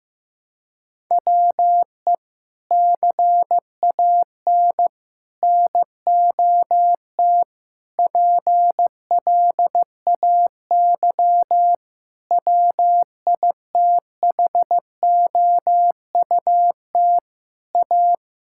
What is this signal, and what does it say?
Morse code, heard as a keyed tone, about 15 wpm, 700 Hz. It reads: WE CAN NOT PLAY WITHOUT A
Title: Sentences from Top 1000 Words - Review Part 6 - 15wpm